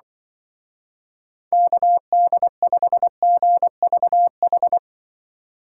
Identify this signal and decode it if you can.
Morse code, heard as a keyed tone, about 25 wpm, 700 Hz. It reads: KD5GVH
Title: Speed Racing - USA 2x3 Call Signs 20wpm